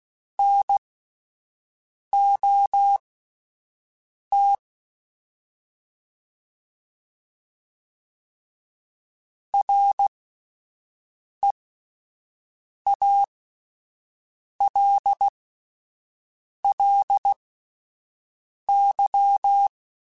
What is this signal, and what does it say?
Morse code, heard as a keyed tone, about 16 wpm, spaced as for 5 wpm, 800 Hz. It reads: NOT REALLY
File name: (spaced) / QSOs Spaced-mp3s / WD0BGP 1 @10 sp